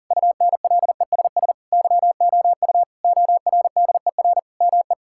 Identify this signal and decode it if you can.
Morse code, heard as a keyed tone, about 40 wpm, 700 Hz. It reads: UNLESS YOU ORDER ME
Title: Sentences from Top 1000 Words - Review Part 6 - 40wpm